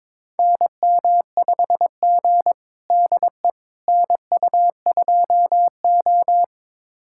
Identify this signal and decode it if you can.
Morse code, heard as a keyed tone, about 22 wpm, 700 Hz. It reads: NM5G DE NU2O
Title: USA Callsign DE Callsign - 4 Characters - 22wpm